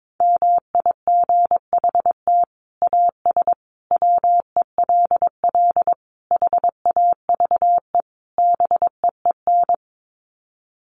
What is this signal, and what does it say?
Morse code, heard as a keyed tone, about 22 wpm, 700 Hz. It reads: MIGHT AS WELL HAVE BEEN
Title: Sets of 5 Words Encore2 22wpm